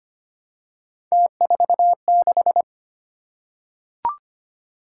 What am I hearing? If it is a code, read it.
Morse code, heard as a keyed tone, about 25 wpm, 700 Hz. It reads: T46
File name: Three Letter-Number Combinations - 25wpm